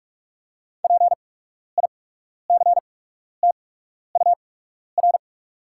Morse code, heard as a keyed tone, about 48 wpm, spaced as for 12 wpm, 690 Hz. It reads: PICTUR
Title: 8x Character Spacing - Top 300 Words - 45wpm